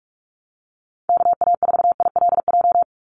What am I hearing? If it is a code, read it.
Morse code, heard as a keyed tone, about 44 wpm, 700 Hz. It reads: KA4ILJ
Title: USA Call Signs 45wpm